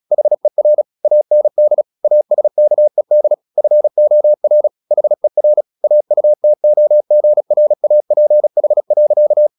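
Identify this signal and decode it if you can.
Morse code, heard as a keyed tone, about 36 wpm, 600 Hz. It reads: HER AND ASKED FOR HER AUTOGRAPH.
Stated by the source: Morse practice at 36 wpm, text of Ken Follett's novel Never, from CW med Gnister.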